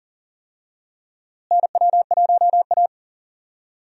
Morse code, heard as a keyed tone, about 40 wpm, 700 Hz. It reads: NW1A
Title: USA Call Signs - 4 Characters - Repeated 3x - 40wpm